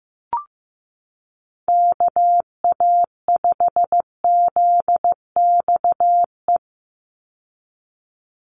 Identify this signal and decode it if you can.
Morse code, heard as a keyed tone, about 15 wpm, 700 Hz. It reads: KA5ZXE
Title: USA Call Signs 15wpm